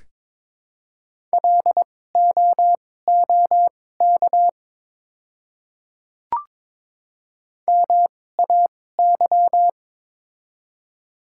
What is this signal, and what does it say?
Morse code, heard as a keyed tone, about 22 wpm, spaced as for 16 wpm, 700 Hz. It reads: LOOK MAY
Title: 2x Character Spacing - Top 100 Words - 22wpm